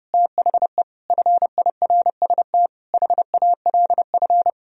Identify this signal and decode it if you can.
Morse code, heard as a keyed tone, about 35 wpm, 700 Hz. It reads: THE FIRST HALF
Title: Speed-Racing - Sets of 5 Words 30wpm